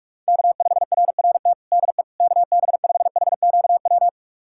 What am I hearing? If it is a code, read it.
Morse code, heard as a keyed tone, about 41 wpm, 700 Hz. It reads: K5RRT DE KB5HQW
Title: USA Callsign DE Callsign 45wpm